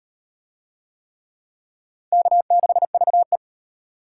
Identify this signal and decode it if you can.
Morse code, heard as a keyed tone, about 38 wpm, 700 Hz. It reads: K6VE